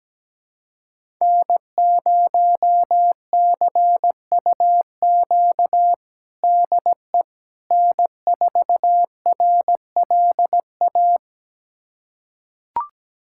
Morse code, heard as a keyed tone, about 17 wpm, 700 Hz. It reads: N0CUQ DE N4RLA
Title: USA Callsign DE Callsign - 5 Characters - 17wpm